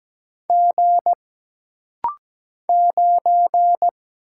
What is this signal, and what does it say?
Morse code, heard as a keyed tone, about 17 wpm, 700 Hz. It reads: G 9
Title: Single Letter-Number - Rapid-Fire 17wpm